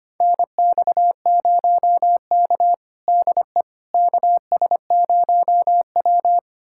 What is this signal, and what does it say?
Morse code, heard as a keyed tone, about 25 wpm, 700 Hz. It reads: NX0K DE KS0W
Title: USA Callsign DE Callsign - 4 Characters - 25wpm